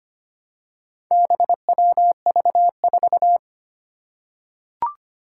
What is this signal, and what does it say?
Morse code, heard as a keyed tone, about 25 wpm, 700 Hz. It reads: BWV4
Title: Four Challenging Letter-Number Combinations 25wpm